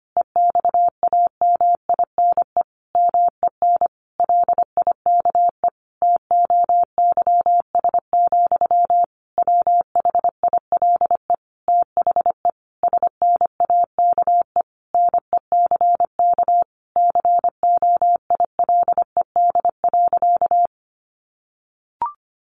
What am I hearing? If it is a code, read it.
Morse code, heard as a keyed tone, about 25 wpm, 700 Hz. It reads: EXAMINE MEN LIKE TOYS, WHILE THE SNAKE NECK COILED.
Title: A Sound Of Thunder 25wpm